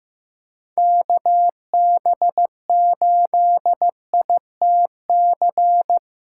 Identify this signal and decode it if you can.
Morse code, heard as a keyed tone, about 15 wpm, 700 Hz. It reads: KB8ITC
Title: USA Callsign DE Callsign - 6 Characters - 15wpm